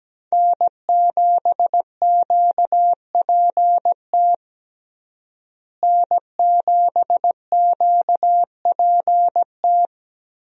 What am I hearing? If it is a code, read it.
Morse code, heard as a keyed tone, about 17 wpm, 700 Hz. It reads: N7QPT N7QPT
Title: USA Call Signs - 5 Characters - Repeated 3x - 17wpm